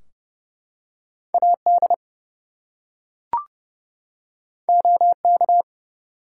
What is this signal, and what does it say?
Morse code, heard as a keyed tone, about 30 wpm, 700 Hz. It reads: AD OK
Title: Two Letter Words - 30wpm